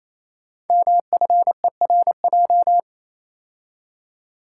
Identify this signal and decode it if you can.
Morse code, heard as a keyed tone, about 29 wpm, 700 Hz. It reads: MFERJ
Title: Code Groups - 5 Letters - 28wpm